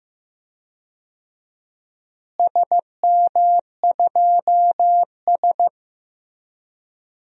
Morse code, heard as a keyed tone, about 15 wpm, 700 Hz. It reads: SM2S
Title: DXCC-100 Call Signs - 4 Characters - Repeated 3x - 15wpm